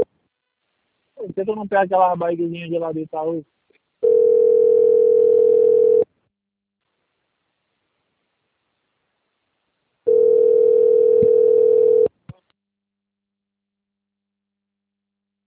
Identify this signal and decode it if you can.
Ringback tone (North America), 440 + 480 Hz, with speech.
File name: rg-1600-unknown- - - .9108